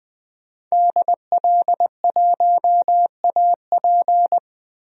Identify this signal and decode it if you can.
Morse code, heard as a keyed tone, about 20 wpm, 700 Hz. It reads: DL1AP
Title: DXCC-100 Call Signs - 5 Characters - Repeated 2x - 20wpm